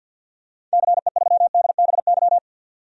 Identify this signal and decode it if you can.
Morse code, heard as a keyed tone, about 52 wpm, 700 Hz. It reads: CE3DBY
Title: DXCC-100 Call Signs - 6 Characters - 50wpm